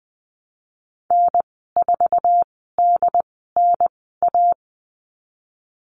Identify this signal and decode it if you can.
Morse code, heard as a keyed tone, about 20 wpm, spaced as for 15 wpm, 700 Hz. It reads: N4DNA
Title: 076 Lesson 13g - Letter L - Call Signs 20wpm